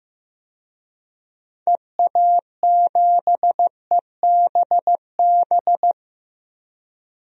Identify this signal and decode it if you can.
Morse code, heard as a keyed tone, about 15 wpm, 700 Hz. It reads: EA7EBB
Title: DXCC-100 Call Signs - 6 Characters - 15wpm